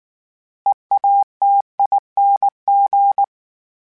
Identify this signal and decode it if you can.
Morse code, heard as a keyed tone, about 19 wpm, 800 Hz. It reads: EATING
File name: Special INGs / Special INGs-mp3s / Sngl Ing @18